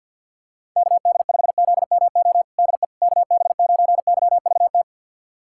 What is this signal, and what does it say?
Morse code, heard as a keyed tone, about 52 wpm, 700 Hz. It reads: KD5ZMO DE KB9YVT
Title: USA Callsign DE Callsign - 6 Characters - 50wpm